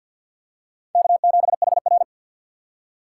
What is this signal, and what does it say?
Morse code, heard as a keyed tone, about 51 wpm, 700 Hz. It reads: K7HR
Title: USA Call Signs - 4 Characters - 2x Character Spacing - 50wpm